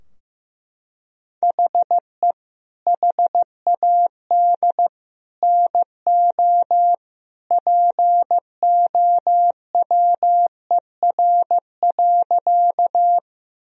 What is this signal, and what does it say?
Morse code, heard as a keyed tone, about 15 wpm, 700 Hz. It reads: HE HAD NO POWER.
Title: Sentences from Top 1000 Words - Review Part 2 - 15wpm